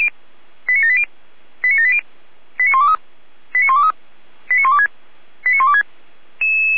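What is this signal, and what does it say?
Five-tone selective call (ZVEI), heard as quick runs of five tones, tones of about 80 ms. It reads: 89780 89780 89123 89123 89126 89126
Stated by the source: Five-tone ZVEI selective call received with an SDR, Feuerwehr Orlamünde.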